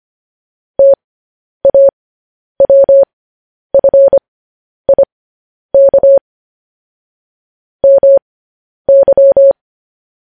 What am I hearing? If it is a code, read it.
Morse code, heard as a keyed tone, about 25 wpm, spaced as for 10 wpm, 550 Hz. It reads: TAWFIK MY
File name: podcast / News headlines-1x-25x10